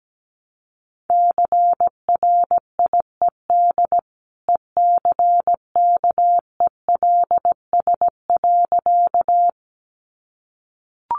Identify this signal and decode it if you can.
Morse code, heard as a keyed tone, about 17 wpm, 700 Hz. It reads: CRIED ECKELS.
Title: A Sound Of Thunder 17wpm